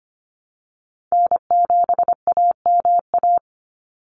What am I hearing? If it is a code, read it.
Morse code, heard as a keyed tone, about 25 wpm, 700 Hz. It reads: N7AMA
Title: NA SOTA Chaser Callsigns 25wpm